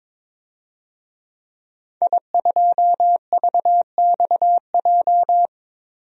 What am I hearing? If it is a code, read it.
Morse code, heard as a keyed tone, about 22 wpm, 700 Hz. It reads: I2VXJ
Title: DXCC-100 Call Signs - 5 Characters - 22wpm